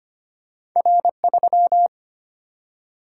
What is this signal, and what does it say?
Morse code, heard as a keyed tone, about 25 wpm, 700 Hz. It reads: R3
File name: Two Letter-Number Combinations - 25wpm